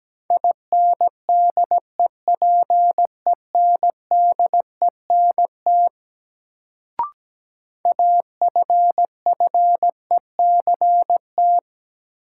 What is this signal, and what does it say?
Morse code, heard as a keyed tone, about 17 wpm, 700 Hz. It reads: INDEPENDENT AFFECT
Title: Sentences from Top 900 Words - New Word Drill - 17wpm